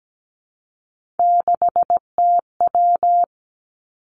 Morse code, heard as a keyed tone, about 17 wpm, 700 Hz. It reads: 6TW
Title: Three Challenging Letter-Number Combinations 17wpm